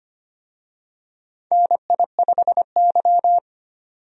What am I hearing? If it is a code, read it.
Morse code, heard as a keyed tone, about 25 wpm, 700 Hz. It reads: NI5Y
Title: USA Call Signs - 4 Characters - 25wpm